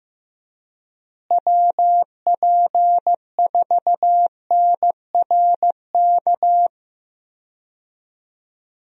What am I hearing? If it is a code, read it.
Morse code, heard as a keyed tone, about 15 wpm, 700 Hz. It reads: WP4NRK
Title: USA Call Signs - 6 Characters - Repeated 3x - 15wpm